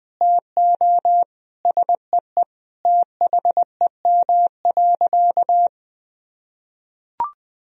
Morse code, heard as a keyed tone, about 20 wpm, 700 Hz. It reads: TO SEE THEM.
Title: Sentences from Top 1000 Words - Review Part 1 - 20wpm